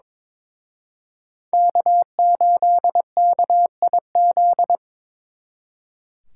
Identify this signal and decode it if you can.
Morse code, heard as a keyed tone, about 26 wpm, 700 Hz. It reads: K8KIZ
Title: Speed Racing - USA Call Signs 22wpm